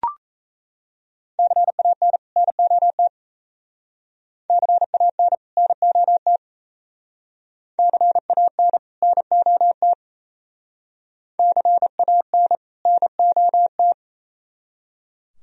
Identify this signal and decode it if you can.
Morse code, heard as a keyed tone, about 34 wpm, 700 Hz. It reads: CAN NOT CAN NOT CAN NOT CAN NOT
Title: Speed-Racing - Sets of 2 Words 28wpm